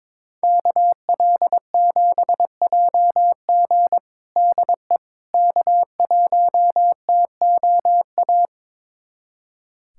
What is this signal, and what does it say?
Morse code, heard as a keyed tone, about 22 wpm, 700 Hz. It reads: KL7JG DE K1TOA
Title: USA Callsign DE Callsign - 5 Characters - 22wpm